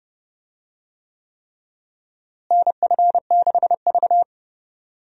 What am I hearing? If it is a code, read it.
Morse code, heard as a keyed tone, about 30 wpm, 700 Hz. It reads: NF6V
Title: USA Call Signs - 4 Characters - Repeated 3x - 30wpm